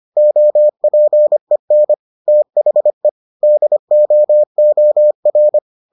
Morse code, heard as a keyed tone, about 25 wpm, 600 Hz. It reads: OPEN THE DOOR